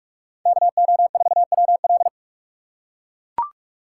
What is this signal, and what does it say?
Morse code, heard as a keyed tone, about 43 wpm, 700 Hz. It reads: KO4WL
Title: USA Call Signs - 5 Characters - 3x Character Spacing - 45wpm